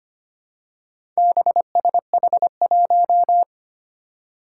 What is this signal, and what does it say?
Morse code, heard as a keyed tone, about 25 wpm, 700 Hz. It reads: BSH1
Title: Four Challenging Letter-Number Combinations 25wpm